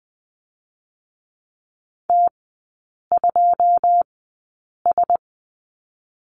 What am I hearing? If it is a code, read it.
Morse code, heard as a keyed tone, about 20 wpm, 700 Hz. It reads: T 2 S
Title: 121 Lesson 20e - Number 3 - Three Characters 20wpm